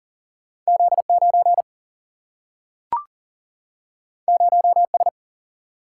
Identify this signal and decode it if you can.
Morse code, heard as a keyed tone, about 40 wpm, 700 Hz. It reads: Z9 0S
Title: Single-Letter Single-Number 40wpm